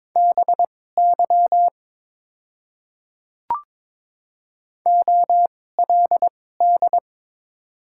Morse code, heard as a keyed tone, about 22 wpm, spaced as for 16 wpm, 700 Hz. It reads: BY OLD
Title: 2x Character Spacing - Top 200 Words - 22wpm